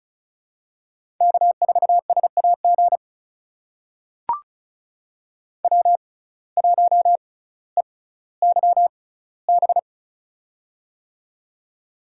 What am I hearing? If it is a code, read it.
Morse code, heard as a keyed tone, about 36 wpm, spaced as for 18 wpm, 700 Hz. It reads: K4SAG W1EYB
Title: USA Call Signs - 6x Character Spacing - 35wpm